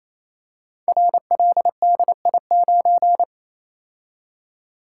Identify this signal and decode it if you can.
Morse code, heard as a keyed tone, about 27 wpm, 700 Hz. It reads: RLDI9
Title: Code Groups - 5 Letter:Numbers - 28wpm